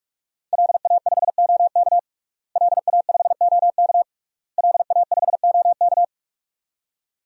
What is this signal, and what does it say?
Morse code, heard as a keyed tone, about 46 wpm, 700 Hz. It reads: LA5OK LA5OK LA5OK
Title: DXCC-100 Call Signs - 5 Characters - Repeated 3x - 45wpm